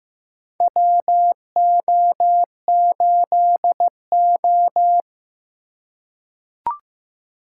Morse code, heard as a keyed tone, about 15 wpm, 700 Hz. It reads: WO8O
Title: USA Call Signs - 4 Characters - 4x Character Spacing - 15wpm